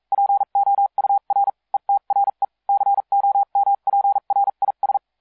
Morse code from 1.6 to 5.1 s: ETRE COMPRIS